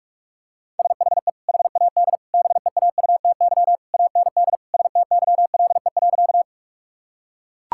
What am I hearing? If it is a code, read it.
Morse code, heard as a keyed tone, about 42 wpm, 700 Hz. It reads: SHE HAD BEAUTY AND STYLE.
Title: Sentences from Top 1000 Words - Review Part 5 - 45wpm